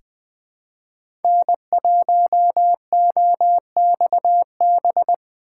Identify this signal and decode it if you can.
Morse code, heard as a keyed tone, about 20 wpm, 700 Hz. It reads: N1OXB
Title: USA Call Signs - 5 Characters - 4x Character Spacing - 20wpm